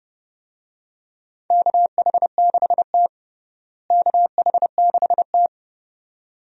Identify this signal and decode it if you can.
Morse code, heard as a keyed tone, about 30 wpm, 700 Hz. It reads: KH6T KH6T
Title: DXCC-100 Call Signs - 4 Characters - Repeated 2x - 30wpm